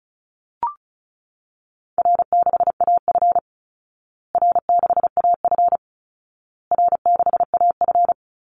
Morse code, heard as a keyed tone, about 34 wpm, 700 Hz. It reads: R6AF R6AF R6AF
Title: Easy Call Signs Repeated 3x - 35wpm